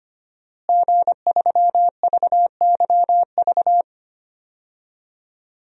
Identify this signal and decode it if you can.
Morse code, heard as a keyed tone, about 25 wpm, 700 Hz. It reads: G3VYV
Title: DXCC-100 Call Signs - 5 Characters - Repeated 3x - 25wpm